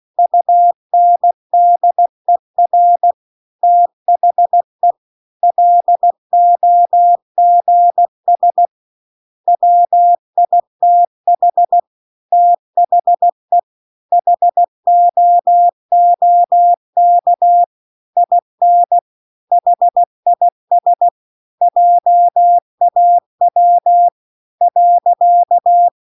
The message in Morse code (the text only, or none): UNDER THE LOGS WITH THE HOOK IN HIS JAW .